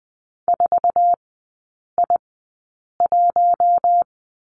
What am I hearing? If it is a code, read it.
Morse code, heard as a keyed tone, about 20 wpm, spaced as for 8 wpm, 700 Hz. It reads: 4I1
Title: 121 Lesson 20e - Number 3 - Three Characters 20wpm